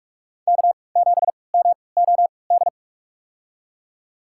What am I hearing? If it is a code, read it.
Morse code, heard as a keyed tone, about 47 wpm, spaced as for 26 wpm, 700 Hz. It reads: K7MOD
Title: USA Call Signs - 3x Character Spacing - 45wpm